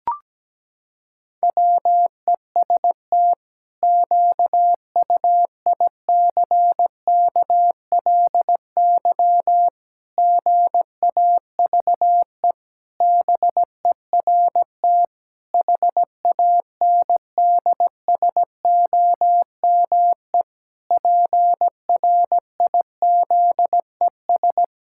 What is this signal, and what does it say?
Morse code, heard as a keyed tone, about 17 wpm, 700 Hz. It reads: WEST QUICKLY GAVE BERT HANDSOME PRIZES
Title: Pangram Challenge v2 - 17wpm